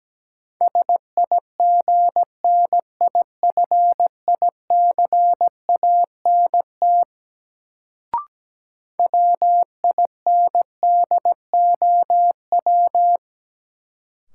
Morse code, heard as a keyed tone, about 17 wpm, 700 Hz. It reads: SIGNIFICANT WINDOW
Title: Sentences from Top 1600 Words - New Word Drill Alt - 17wpm